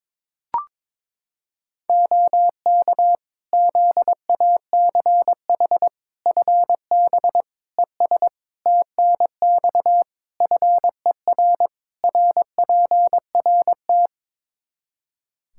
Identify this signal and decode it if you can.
Morse code, heard as a keyed tone, about 22 wpm, 700 Hz. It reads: OK ZACH FB ES TNX FER RPRT
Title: K1USN SST QSO Practice 22wpm